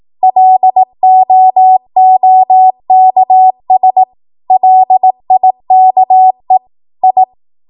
Morse code, heard as a keyed tone, about 18 wpm, 750 Hz. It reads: LOOKS LIKE I
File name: Archive / 18 WPM / 18WPM